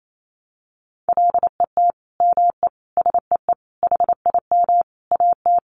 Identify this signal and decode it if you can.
Morse code, heard as a keyed tone, about 28 wpm, 700 Hz. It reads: LET ME SEE HIM AT
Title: Sentences from Top 300 Words 28wpm